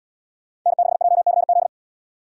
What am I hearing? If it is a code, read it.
Morse code, heard as a keyed tone, about 71 wpm, 700 Hz. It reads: N6JCZ